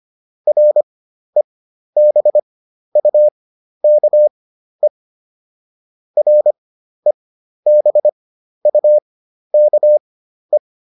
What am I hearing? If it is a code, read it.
Morse code, heard as a keyed tone, about 25 wpm, spaced as for 12 wpm, 600 Hz. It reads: REBUKE REBUKE